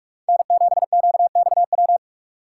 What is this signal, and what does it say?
Morse code, heard as a keyed tone, about 54 wpm, 700 Hz. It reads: N7QXW
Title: Speed Racing - USA 1x3 Call Signs 45wpm